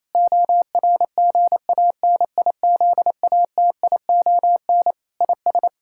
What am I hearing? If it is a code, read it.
Morse code, heard as a keyed tone, about 28 wpm, 700 Hz. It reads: ORGANIZATION IS
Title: Sentences from Top 1000 Words - Review Part 5 - 28wpm